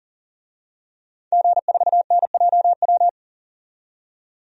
Morse code, heard as a keyed tone, about 40 wpm, 700 Hz. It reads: G4NJW
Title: DXCC-100 Call Signs - 5 Characters - 40wpm